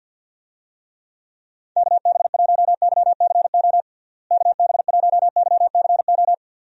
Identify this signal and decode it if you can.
Morse code, heard as a keyed tone, about 45 wpm, 700 Hz. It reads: KB1YCO KB1YCO
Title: USA Call Signs - 6 Characters - Repeated 2x - 50wpm